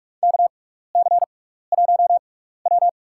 Morse code, heard as a keyed tone, about 42 wpm, spaced as for 15 wpm, 700 Hz. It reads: KC1W
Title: USA Call Signs - 4 Characters - 6x Character Spacing - 45wpm